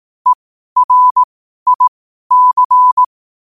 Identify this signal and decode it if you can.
Morse code, heard as a keyed tone, about 18 wpm, spaced as for 12 wpm, 1000 Hz. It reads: ERIC
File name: qotdfiles / 12wpm